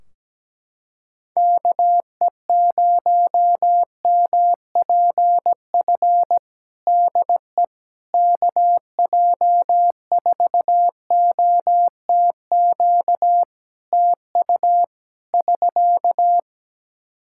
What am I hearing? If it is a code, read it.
Morse code, heard as a keyed tone, about 17 wpm, 700 Hz. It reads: KE0MPF DE KJ4OTQ TU <SK>
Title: Practice QSO - Exchange 4 - Ending - 17wpm